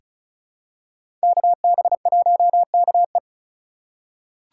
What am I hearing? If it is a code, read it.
Morse code, heard as a keyed tone, about 36 wpm, 700 Hz. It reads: KB1KE
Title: USA Call Signs - 5 Characters - 6x Character Spacing - 35wpm